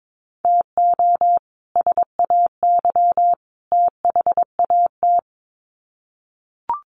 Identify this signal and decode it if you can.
Morse code, heard as a keyed tone, about 22 wpm, 700 Hz. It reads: TO SAY THAT
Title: Sets of 3 Words 22wpm